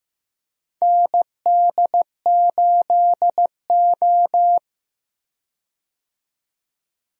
Morse code, heard as a keyed tone, about 15 wpm, 700 Hz. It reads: ND8O
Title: USA Call Signs - 4 Characters - 15wpm